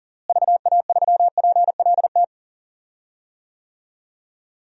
Morse code, heard as a keyed tone, about 40 wpm, 700 Hz. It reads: VA3PLT
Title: DXCC-100 Call Signs - 6 Characters - Repeated 3x - 40wpm